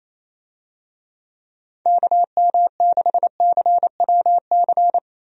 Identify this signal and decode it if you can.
Morse code, heard as a keyed tone, about 27 wpm, 700 Hz. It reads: KM6CWC
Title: USA Call Signs - 6 Characters - 28wpm